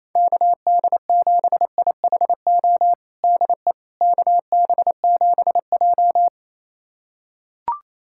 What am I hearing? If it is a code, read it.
Morse code, heard as a keyed tone, about 28 wpm, 700 Hz. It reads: KD7IHO DE KB7J
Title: USA Callsign DE Callsign 28wpm